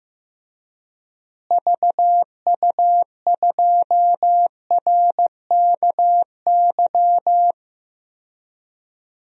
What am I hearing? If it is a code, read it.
Morse code, heard as a keyed tone, about 15 wpm, 700 Hz. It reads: VU2RKY
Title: DXCC-100 Call Signs - 6 Characters - Repeated 3x - 15wpm